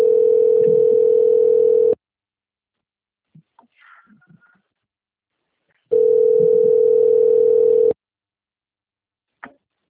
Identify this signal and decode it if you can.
Ringback tone (North America), 440 + 480 Hz.